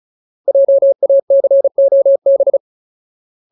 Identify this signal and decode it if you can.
Morse code, heard as a keyed tone, about 34 wpm, 550 Hz. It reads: JACOB
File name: Ultimate Word List / 35WPM Ultimate